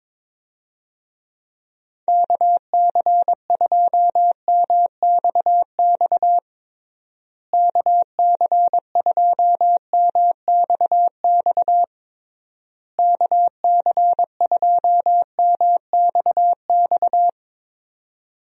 Morse code, heard as a keyed tone, about 22 wpm, 700 Hz. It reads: KC2MXX KC2MXX KC2MXX
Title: USA Call Signs - 6 Characters - Repeated 3x - 22wpm